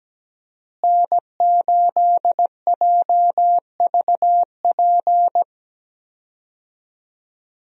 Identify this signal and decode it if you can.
Morse code, heard as a keyed tone, about 17 wpm, 700 Hz. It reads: N8JVP
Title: USA Call Signs - 5 Characters - Repeated 2x - 17wpm